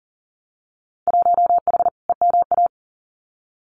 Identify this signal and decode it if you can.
Morse code, heard as a keyed tone, about 38 wpm, 700 Hz. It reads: 1H EMA
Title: Winter Field Day Exchanges 40wpm